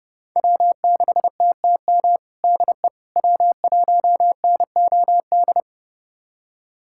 Morse code, heard as a keyed tone, about 32 wpm, 700 Hz. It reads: W6TTM DE W1NOD
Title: USA Callsign DE Callsign - 5 Characters - 30wpm